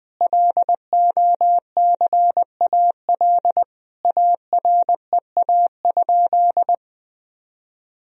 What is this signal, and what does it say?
Morse code, heard as a keyed tone, about 20 wpm, 700 Hz. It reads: LOCAL AREA?
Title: Sentences from Top 1000 Words - Review Part 3 - 20wpm